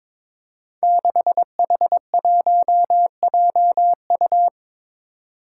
Morse code, heard as a keyed tone, about 23 wpm, 700 Hz. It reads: 6H1JU
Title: Five Challenging Letter-Number Combinations 22wpm